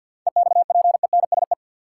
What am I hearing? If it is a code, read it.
Morse code, heard as a keyed tone, about 52 wpm, 700 Hz. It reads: EXPENSE
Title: Sets of 4 Words - New Word Drill - 50wpm